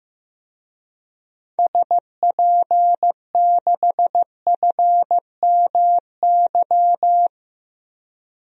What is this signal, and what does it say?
Morse code, heard as a keyed tone, about 15 wpm, 700 Hz. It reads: SP6FMY
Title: DXCC-100 Call Signs - 6 Characters - 15wpm